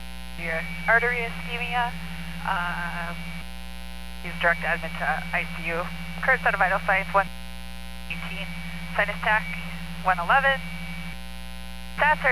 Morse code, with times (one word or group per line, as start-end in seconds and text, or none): none